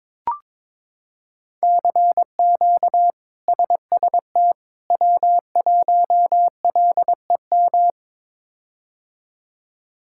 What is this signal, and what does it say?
Morse code, heard as a keyed tone, about 21 wpm, 700 Hz. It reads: CQ SST W1LEM
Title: K1USN SST QSO Practice - 22wpm